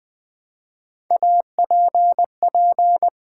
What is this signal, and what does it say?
Morse code, heard as a keyed tone, about 20 wpm, 700 Hz. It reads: APP
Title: Build A Word - 1 Letter At A Time - 3 to 5-Letter Words - 20wpm